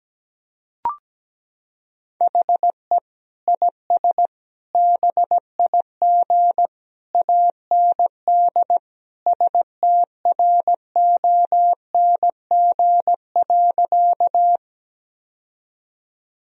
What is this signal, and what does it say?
Morse code, heard as a keyed tone, about 17 wpm, 700 Hz. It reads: HE IS BIG AND STRONG.
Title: Sentences from Top 1000 Words - Review Part 3 - 17wpm